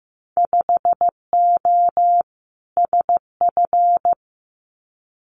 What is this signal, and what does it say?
Morse code, heard as a keyed tone, about 15 wpm, 700 Hz. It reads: 5O SF
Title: Winter Field Day Exchanges Encore 15wpm